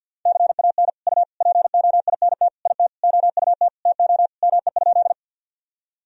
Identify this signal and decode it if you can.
Morse code, heard as a keyed tone, about 50 wpm, 700 Hz. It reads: CAN U POINT IT OUT TO ME?
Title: Sentences from Top 200 Words and Abbreviations 50wpm